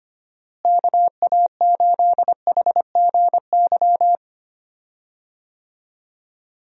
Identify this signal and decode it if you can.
Morse code, heard as a keyed tone, about 25 wpm, 700 Hz. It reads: KA8HGY
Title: USA Call Signs - 6 Characters - Repeated 2x - 25wpm